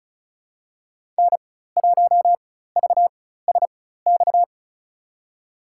Morse code, heard as a keyed tone, about 34 wpm, spaced as for 16 wpm, 700 Hz. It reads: N1VSX
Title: USA Call Signs - 5 Characters - 4x Character Spacing - 35wpm